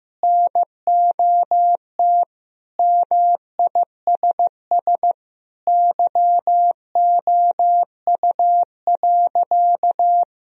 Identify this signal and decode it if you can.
Morse code, heard as a keyed tone, about 16 wpm, 700 Hz. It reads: NOT MISS YOU.
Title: Sentences from Top 1000 Words - Review Part 3 - 15wpm